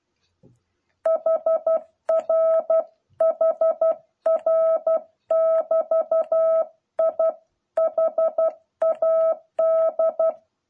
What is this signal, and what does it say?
Morse code, heard as a keyed tone, about 12 wpm, 650 Hz. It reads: HR HR <BT> I HAD